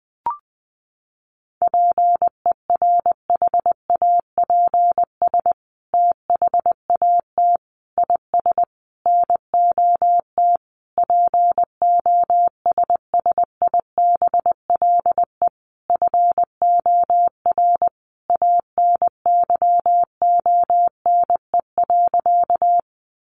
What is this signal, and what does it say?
Morse code, heard as a keyed tone, about 20 wpm, 700 Hz. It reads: PERHAPS THAT IS NOT POSSIBLE FOR ANYONE.
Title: Sentences from Top 900 Words 20wpm